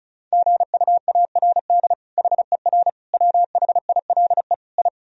Morse code, heard as a keyed tone, about 36 wpm, 700 Hz. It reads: GUARD HER WHILE I
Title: Sentences from Top 1000 Words - Review Part 4 - 35wpm